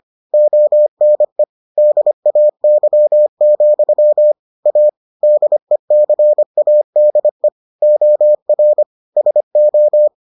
Morse code, heard as a keyed tone, about 25 wpm, 600 Hz. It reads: ONE DAY, A DECADE OR SO